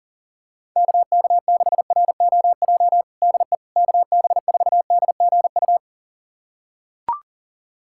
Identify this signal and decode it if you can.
Morse code, heard as a keyed tone, about 37 wpm, 700 Hz. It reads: KK6ROJ DE KB4DGU
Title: USA Callsign DE Callsign - 6 Characters - 40wpm